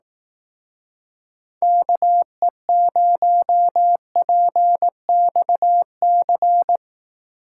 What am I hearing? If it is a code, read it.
Morse code, heard as a keyed tone, about 18 wpm, 700 Hz. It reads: KE0PXC
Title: Speed Racing - USA 2x3 Call Signs 15wpm